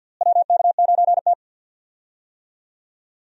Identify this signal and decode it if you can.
Morse code, heard as a keyed tone, about 51 wpm, 700 Hz. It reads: WK9T
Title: USA Call Signs - 4 Characters - Repeated 2x - 50wpm